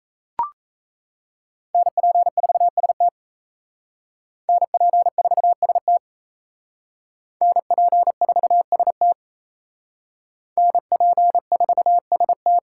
Morse code, heard as a keyed tone, about 37 wpm, 700 Hz. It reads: NP4ST NP4ST NP4ST NP4ST